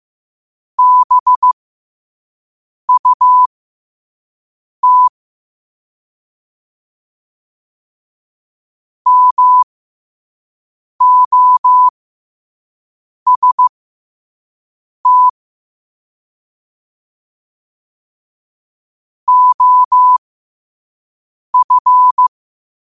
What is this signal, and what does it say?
Morse code, heard as a keyed tone, about 15 wpm, spaced as for 5 wpm, 1000 Hz. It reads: BUT MOST OF
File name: qotdfiles / 5wpm